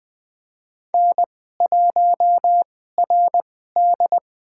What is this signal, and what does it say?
Morse code, heard as a keyed tone, about 20 wpm, spaced as for 14 wpm, 700 Hz. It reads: N1RD
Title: USA Call Signs - 4 Characters - 2x Character Spacing - 20wpm